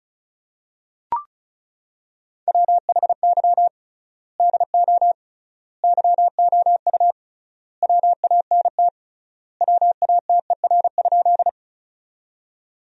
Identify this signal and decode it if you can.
Morse code, heard as a keyed tone, about 36 wpm, 700 Hz. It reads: WHY DO YOU WANT WATER?
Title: 3x Word-Spacing - Sentences from Top 200 Words - 35wpm